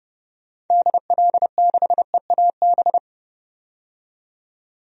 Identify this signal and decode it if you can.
Morse code, heard as a keyed tone, about 31 wpm, 700 Hz. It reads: DL6EAB
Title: DXCC-100 Call Signs - 6 Characters - 30wpm